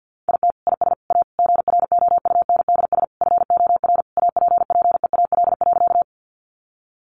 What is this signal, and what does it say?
Morse code, heard as a keyed tone, about 50 wpm, 700 Hz. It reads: IT IS A GROUNDS FOR APPEAL.
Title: Sentences from Top 1300 Words 50wpm